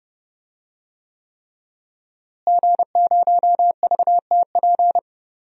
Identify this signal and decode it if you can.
Morse code, heard as a keyed tone, about 30 wpm, 700 Hz. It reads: G0VTP
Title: DXCC-100 Call Signs - 5 Characters - Repeated 3x - 30wpm